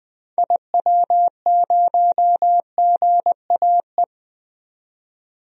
Morse code, heard as a keyed tone, about 20 wpm, 700 Hz. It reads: IW0GAE
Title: DXCC-100 Call Signs - 6 Characters - 20wpm